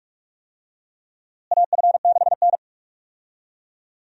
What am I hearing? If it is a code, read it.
Morse code, heard as a keyed tone, about 43 wpm, 700 Hz. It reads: AF6N
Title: USA Call Signs - 4 Characters - Repeated 2x - 45wpm